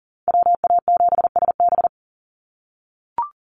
Morse code, heard as a keyed tone, about 40 wpm, 700 Hz. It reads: WA7SB
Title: NA SOTA Chaser Callsigns 40wpm